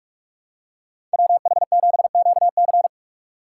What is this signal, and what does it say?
Morse code, heard as a keyed tone, about 44 wpm, 700 Hz. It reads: WH7QC